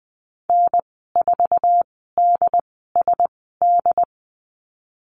Morse code, heard as a keyed tone, about 20 wpm, spaced as for 15 wpm, 700 Hz. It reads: N4DSD
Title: 122 Lesson 20g - Number 3 - Call Signs 20wpm